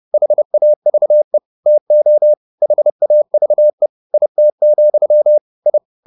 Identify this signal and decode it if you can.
Morse code, heard as a keyed tone, about 30 wpm, 600 Hz. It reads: HAVE TO HAVE IT, I
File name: the old man and the sea / the old man and the sea 1 30wpm